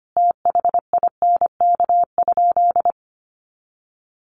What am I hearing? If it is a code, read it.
Morse code, heard as a keyed tone, about 25 wpm, 700 Hz. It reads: THINK?
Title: Sentences from Top 200 Words 25wpm